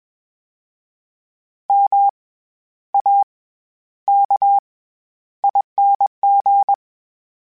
Morse code, heard as a keyed tone, about 21 wpm, spaced as for 11 wpm, 800 Hz. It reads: MAKING
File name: Special INGs-mp3s / Special INGs-mp3s / SlowToFaster ing @20